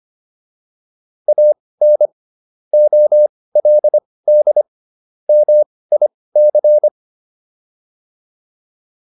Morse code, heard as a keyed tone, about 25 wpm, spaced as for 18 wpm, 600 Hz. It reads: AN OLD MIC